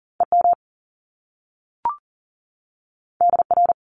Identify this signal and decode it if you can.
Morse code, heard as a keyed tone, about 40 wpm, 700 Hz. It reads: EM DR
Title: Top Two Letter Combinations in Words 40wpm